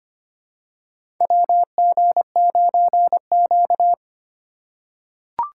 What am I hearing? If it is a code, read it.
Morse code, heard as a keyed tone, about 25 wpm, 700 Hz. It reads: WG9Q
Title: USA Call Signs - 3x Character Spacing - 25wpm